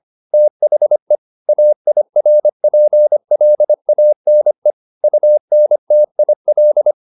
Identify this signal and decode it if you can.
Morse code, heard as a keyed tone, about 25 wpm, 600 Hz. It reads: THE AIRPLANE UNTIL